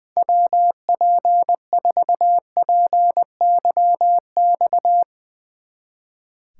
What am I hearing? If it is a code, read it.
Morse code, heard as a keyed tone, about 22 wpm, 700 Hz. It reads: WP4PYX
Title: Speed Racing - USA 2x3 Call Signs 20wpm